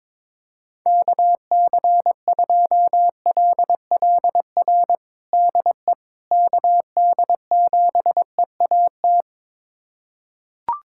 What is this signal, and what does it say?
Morse code, heard as a keyed tone, about 22 wpm, 700 Hz. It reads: KC2LLR DE KD7EAT
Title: USA Callsign DE Callsign - 6 Characters - 22wpm